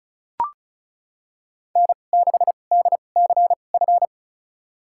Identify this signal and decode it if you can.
Morse code, heard as a keyed tone, about 34 wpm, spaced as for 25 wpm, 700 Hz. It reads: N6DCF
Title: USA Call Signs - 5 Characters - 2x Character Spacing - 35wpm